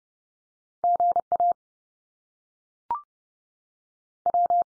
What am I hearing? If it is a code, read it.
Morse code, heard as a keyed tone, about 30 wpm, 700 Hz. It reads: GA W
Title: US States 30WPM